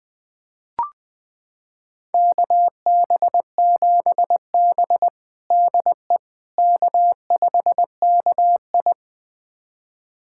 Morse code, heard as a keyed tone, about 20 wpm, 700 Hz. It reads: KB7B DE K5KI
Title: USA Callsign DE Callsign - 4 Characters - 20wpm